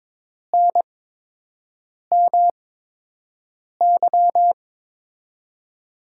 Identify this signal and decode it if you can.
Morse code, heard as a keyed tone, about 22 wpm, spaced as for 6 wpm, 700 Hz. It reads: NMY
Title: USA Call Signs - 5 Characters - 8x Character Spacing - 22wpm